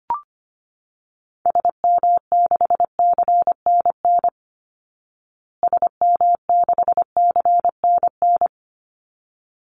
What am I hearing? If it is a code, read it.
Morse code, heard as a keyed tone, about 25 wpm, 700 Hz. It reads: SM6CNN SM6CNN
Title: Call Signs Repeated 3x - 25wpm